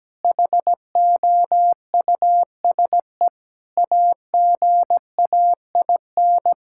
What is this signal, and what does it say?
Morse code, heard as a keyed tone, about 17 wpm, 700 Hz. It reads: HOUSE AGAIN
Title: Sentences from Top 1000 Words - Review Part 6 - 17wpm